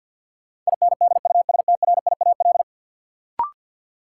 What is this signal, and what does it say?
Morse code, heard as a keyed tone, about 53 wpm, 700 Hz. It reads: INDUSTRIAL
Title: Sets of 5 Words - New Word Drill - 50wpm